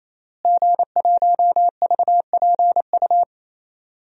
Morse code, heard as a keyed tone, about 28 wpm, 700 Hz. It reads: G1VPU
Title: DXCC-100 Call Signs - 5 Characters - Repeated 3x - 28wpm